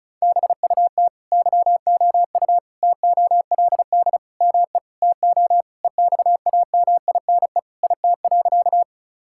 Morse code, heard as a keyed tone, about 34 wpm, 700 Hz. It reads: BUT YOU TOLD ME TO EXAMINE IT.
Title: Sentences from Top 1600 Words 35wpm